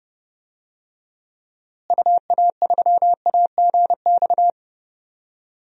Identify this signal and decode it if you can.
Morse code, heard as a keyed tone, about 32 wpm, 700 Hz. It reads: UA3AGX
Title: DXCC-100 Call Signs - 6 Characters - Repeated 3x - 30wpm